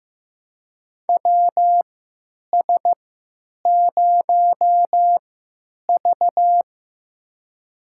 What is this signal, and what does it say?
Morse code, heard as a keyed tone, about 15 wpm, spaced as for 9 wpm, 700 Hz. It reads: WS0V
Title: USA Call Signs - 3x Character Spacing - 15wpm